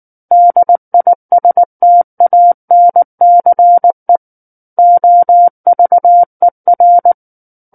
Morse code, heard as a keyed tone, about 19 wpm, 700 Hz. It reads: DISTANCE OVER
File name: Book One / 10-Chapter 10 - 19 WPM